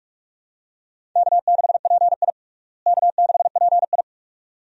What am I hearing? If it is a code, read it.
Morse code, heard as a keyed tone, about 46 wpm, 700 Hz. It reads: K6PI K6PI
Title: USA Call Signs - 4 Characters - Repeated 2x - 45wpm